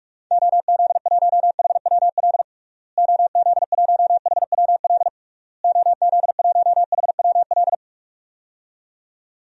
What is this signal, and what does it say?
Morse code, heard as a keyed tone, about 41 wpm, 700 Hz. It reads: OZ1HWL OZ1HWL OZ1HWL
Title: DXCC-100 Call Signs - 6 Characters - Repeated 3x - 45wpm